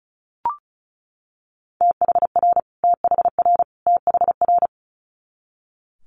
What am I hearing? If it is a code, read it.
Morse code, heard as a keyed tone, about 36 wpm, 700 Hz. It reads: THR THR THR
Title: Multi-letter Phonemes Repeated 3x 35wpm